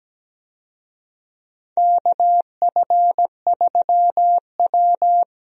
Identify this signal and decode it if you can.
Morse code, heard as a keyed tone, about 17 wpm, 700 Hz. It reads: KF3W
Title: USA Call Signs - 4 Characters - 17wpm